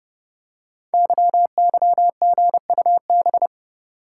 Morse code, heard as a keyed tone, about 30 wpm, 700 Hz. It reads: YYGUB
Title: Code Groups - 5 Letters - 30wpm